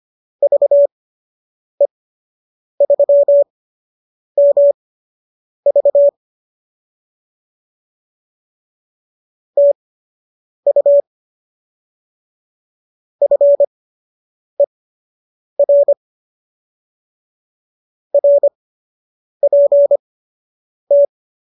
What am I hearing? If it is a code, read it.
Morse code, heard as a keyed tone, about 25 wpm, spaced as for 8 wpm, 570 Hz. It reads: VE3MV TU FER RPT